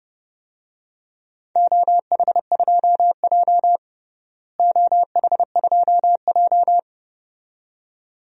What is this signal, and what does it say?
Morse code, heard as a keyed tone, about 32 wpm, 700 Hz. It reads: OH2J OH2J
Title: DXCC-100 Call Signs - 4 Characters - Repeated 2x - 30wpm